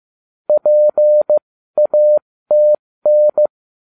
Morse code, heard as a keyed tone, about 15 wpm, spaced as for 13 wpm, 600 Hz. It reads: PATN